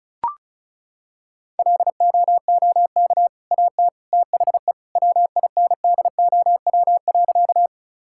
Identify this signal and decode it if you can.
Morse code, heard as a keyed tone, about 36 wpm, 700 Hz. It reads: LOOK AT THE WINDOW.
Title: Sentences from Top 1600 Words 35wpm